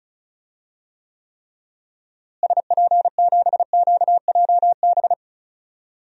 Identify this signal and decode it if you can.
Morse code, heard as a keyed tone, about 35 wpm, 700 Hz. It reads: SP7QJB
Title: DXCC-100 Call Signs - 6 Characters - Repeated 3x - 35wpm